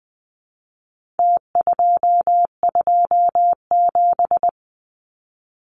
Morse code, heard as a keyed tone, about 20 wpm, 700 Hz. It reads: T227
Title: Numbers - Contest Style - 4-Digit 20wpm